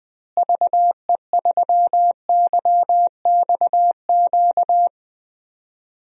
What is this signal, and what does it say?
Morse code, heard as a keyed tone, about 20 wpm, 700 Hz. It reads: VE3YXQ
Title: DXCC-100 Call Signs - 6 Characters - Repeated 3x - 20wpm